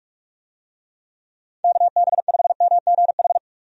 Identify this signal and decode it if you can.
Morse code, heard as a keyed tone, about 46 wpm, 700 Hz. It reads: KB5MGH
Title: USA Call Signs - 6 Characters - 45wpm